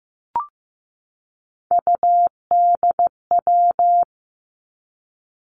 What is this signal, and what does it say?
Morse code, heard as a keyed tone, about 15 wpm, 700 Hz. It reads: UDW
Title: Three Challenging Letter-Number Combinations 15wpm